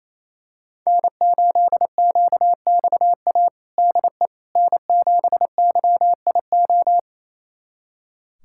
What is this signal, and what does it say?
Morse code, heard as a keyed tone, about 28 wpm, 700 Hz. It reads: N8QXA DE N7YIO
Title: USA Callsign DE Callsign - 5 Characters - 28wpm